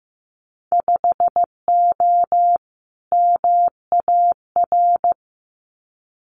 Morse code, heard as a keyed tone, about 15 wpm, 700 Hz. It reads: 5O MAR
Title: Winter Field Day Exchanges Encore 15wpm